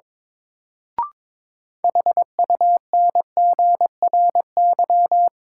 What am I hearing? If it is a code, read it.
Morse code, heard as a keyed tone, about 22 wpm, 700 Hz. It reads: HUNGRY
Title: Sentences from Top 1400 Words - New Word Drill Alt - 22wpm